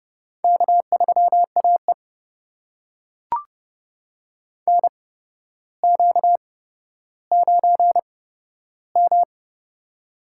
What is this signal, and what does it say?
Morse code, heard as a keyed tone, about 32 wpm, spaced as for 13 wpm, 700 Hz. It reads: K3AE NQ9M
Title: USA Call Signs - 8x Character Spacing - 30wpm